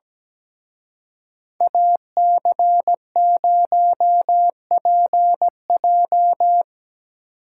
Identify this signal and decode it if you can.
Morse code, heard as a keyed tone, about 20 wpm, 700 Hz. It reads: AC0PJ
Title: Speed Racing - USA Call Signs 17wpm